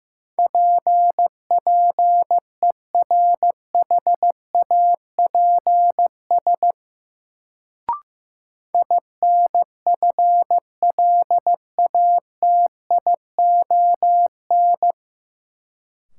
Morse code, heard as a keyed tone, about 15 wpm, 700 Hz. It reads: PPERHAPS INFLATION
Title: Sentences from Top 1100 Words - New Word Drill - 15wpm